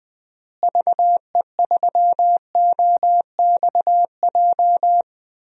Morse code, heard as a keyed tone, about 20 wpm, 700 Hz. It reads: VE3OXJ
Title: DXCC-100 Call Signs - 6 Characters - Repeated 3x - 20wpm